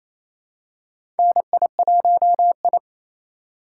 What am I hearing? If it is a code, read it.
Morse code, heard as a keyed tone, about 28 wpm, 700 Hz. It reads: NI1I
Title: USA Call Signs - 4 Characters - 2x Character Spacing - 28wpm